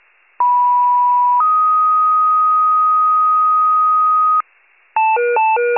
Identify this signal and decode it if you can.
Two-tone page, heard as a one-second tone, then a three-second tone, about 980 then 1250 Hz. It is heard from 0.4 to 4.4 s.